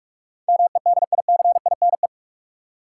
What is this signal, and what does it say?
Morse code, heard as a keyed tone, about 46 wpm, 700 Hz. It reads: MEDICINE
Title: Sentences from Top 900 Words - New Word Drill - 45wpm